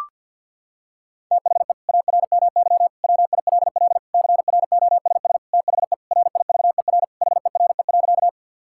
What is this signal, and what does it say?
Morse code, heard as a keyed tone, about 50 wpm, 700 Hz. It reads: THE ARMY WILL CROSS THE RIVER HERE.
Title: Sentences from Top 1000 Words - Review Part 5 - 50wpm